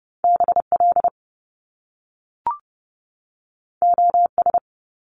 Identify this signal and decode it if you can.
Morse code, heard as a keyed tone, about 30 wpm, 700 Hz. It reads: BL OS
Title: Top Two Letter Combinations in Words 30wpm